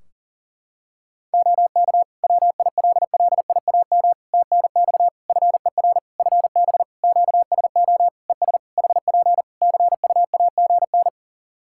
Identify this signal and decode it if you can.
Morse code, heard as a keyed tone, about 40 wpm, 700 Hz. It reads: OK WILLIAM TNX FER FB QSO ES HP CUAGN
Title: Practice QSO - Exchange 4 - Ending - 40wpm